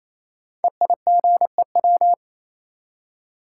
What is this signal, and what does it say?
Morse code, heard as a keyed tone, about 29 wpm, 700 Hz. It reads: EIGEW
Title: Code Groups - 5 Letters - 28wpm